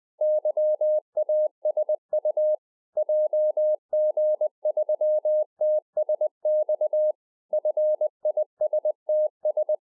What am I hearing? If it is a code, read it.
Morse code, heard as a keyed tone, about 20 wpm, 610 Hz. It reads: YASU JG3TSX FISTS